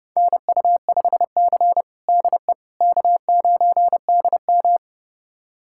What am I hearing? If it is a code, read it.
Morse code, heard as a keyed tone, about 30 wpm, 700 Hz. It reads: NU5C DE K9DM
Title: USA Callsign DE Callsign - 4 Characters - 30wpm